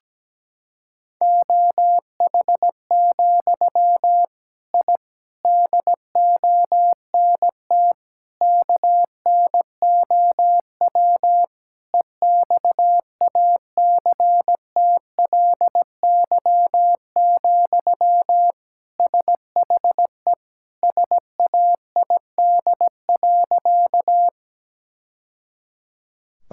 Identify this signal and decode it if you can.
Morse code, heard as a keyed tone, about 17 wpm, 700 Hz. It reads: OH, I DONT KNOW EXACTLY, SHE SAID.
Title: Sentences from Top 1000 Words - Review Part 6 - 17wpm